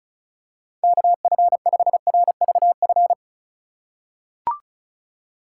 Morse code, heard as a keyed tone, about 36 wpm, 700 Hz. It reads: KF5RVF
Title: USA Call Signs - 6 Characters - 3x Character Spacing - 35wpm